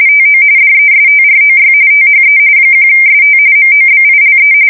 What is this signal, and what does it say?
RTTY, about 45.45 baud. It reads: TO DESTROY THE 1ST MOBILE
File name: fsk170 / USS-Massachusetts